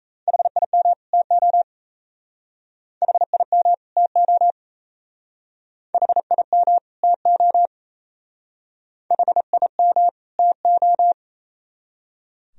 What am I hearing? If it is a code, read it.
Morse code, heard as a keyed tone, about 35 wpm, 690 Hz. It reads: HIM TO HIM TO HIM TO HIM TO
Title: Speed-Racing - Sets of 2 Words 28wpm